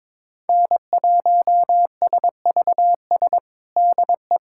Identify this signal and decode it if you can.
Morse code, heard as a keyed tone, about 22 wpm, 700 Hz. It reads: N1SVS DE
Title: USA Callsign DE Callsign - 5 Characters - 22wpm